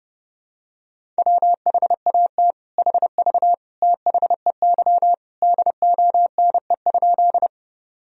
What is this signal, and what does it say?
Morse code, heard as a keyed tone, about 30 wpm, 700 Hz. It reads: WHAT HV THEY DONE?
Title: Sentences from Top 500 Words and Abbreviations 30wpm